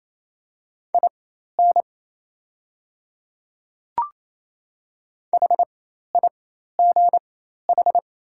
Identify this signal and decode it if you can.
Morse code, heard as a keyed tone, about 28 wpm, spaced as for 13 wpm, 690 Hz. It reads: IN HIGH
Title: 4x Character Spacing - Top 200 Words - 28wpm